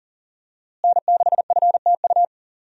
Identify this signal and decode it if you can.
Morse code, heard as a keyed tone, about 38 wpm, 690 Hz. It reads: N6FTU
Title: USA Call Signs - 5 Characters - Repeated 3x - 40wpm